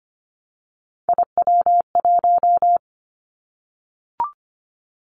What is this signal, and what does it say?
Morse code, heard as a keyed tone, about 25 wpm, 700 Hz. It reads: IW1
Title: Callsign Prefixes for Top 25 DXCC - Repeated 3x - 25wpm